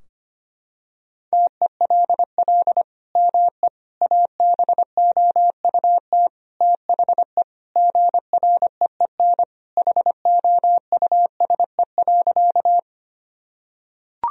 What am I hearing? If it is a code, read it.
Morse code, heard as a keyed tone, about 25 wpm, 700 Hz. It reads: TELL ME ABOUT THE GREEN HOUSE.
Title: Sentences from Top 1000 Words - Review Part 2 - 25wpm